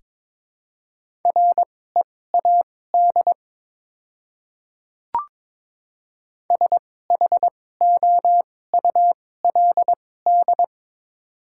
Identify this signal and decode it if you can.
Morse code, heard as a keyed tone, about 22 wpm, spaced as for 16 wpm, 700 Hz. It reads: READ SHOULD
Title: 2x Character Spacing - Top 300 Words - 22wpm